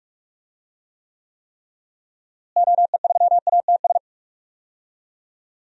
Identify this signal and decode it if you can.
Morse code, heard as a keyed tone, about 41 wpm, 700 Hz. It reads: OE3ATS
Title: DXCC-100 Call Signs - 6 Characters - Repeated 3x - 45wpm